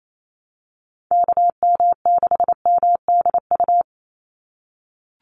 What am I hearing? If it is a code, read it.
Morse code, heard as a keyed tone, about 28 wpm, 700 Hz. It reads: KM6MDU
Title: USA Call Signs 28wpm